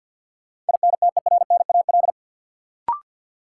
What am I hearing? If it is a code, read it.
Morse code, heard as a keyed tone, about 50 wpm, 700 Hz. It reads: INTERNAL
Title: Sentences from Top 1400 Words - New Word Drill - 50wpm